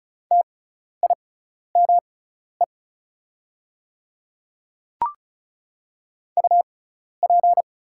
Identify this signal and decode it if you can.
Morse code, heard as a keyed tone, about 34 wpm, spaced as for 12 wpm, 700 Hz. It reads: TIME UP
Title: 6x Character Spacing - Top 100 Words - 35wpm